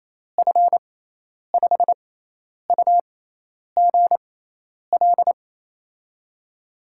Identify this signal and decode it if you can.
Morse code, heard as a keyed tone, about 28 wpm, spaced as for 10 wpm, 700 Hz. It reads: F5UGL
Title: USA Call Signs - 6 Characters - 6x Character Spacing - 28wpm